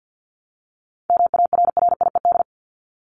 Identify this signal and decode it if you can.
Morse code, heard as a keyed tone, about 50 wpm, 700 Hz. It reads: MARRIED
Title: Top 1501-1750 Words 50wpm